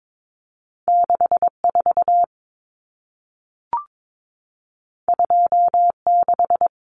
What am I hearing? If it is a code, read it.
Morse code, heard as a keyed tone, about 22 wpm, 700 Hz. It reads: 64 26
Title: Common Cut Numbers - 2-Digit 22wpm